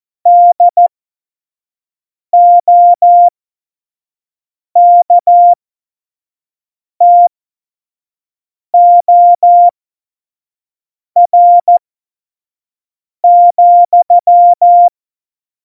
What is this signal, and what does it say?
Morse code, heard as a keyed tone, about 14 wpm, spaced as for 5 wpm, 700 Hz. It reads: DOKTOR,
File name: Morsekurs / Witze / Witz 16. Januar 2018, 25 F70